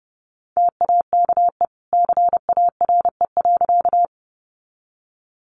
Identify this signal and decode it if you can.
Morse code, heard as a keyed tone, about 30 wpm, 700 Hz. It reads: TAKE CARE.
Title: Sentences from Top 300 Words 30wpm